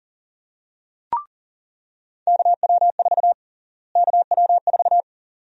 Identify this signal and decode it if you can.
Morse code, heard as a keyed tone, about 40 wpm, 700 Hz. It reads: KW4 KW4
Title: USA Call Sign Prefixes - Repeated 2x - 40wpm